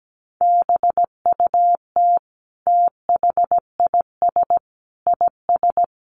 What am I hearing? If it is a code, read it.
Morse code, heard as a keyed tone, about 17 wpm, 700 Hz. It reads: BUT THIS IS
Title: Sentences from Top 200 Words 17wpm